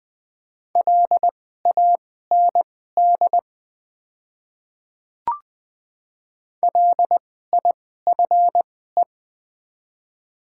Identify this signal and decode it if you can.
Morse code, heard as a keyed tone, about 20 wpm, spaced as for 14 wpm, 690 Hz. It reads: LAND LIFE
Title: 2x Character Spacing - Top 300 Words - 20wpm